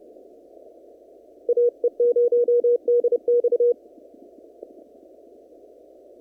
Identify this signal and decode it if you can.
Morse code, heard as a keyed tone, about 30 wpm, 480 Hz. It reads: AE0DX
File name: CW / kt5j wpx cw-op-k5pi